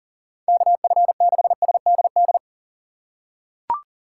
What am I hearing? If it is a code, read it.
Morse code, heard as a keyed tone, about 40 wpm, 700 Hz. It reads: KF6SDD
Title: USA Call Signs - 3x Character Spacing - 40wpm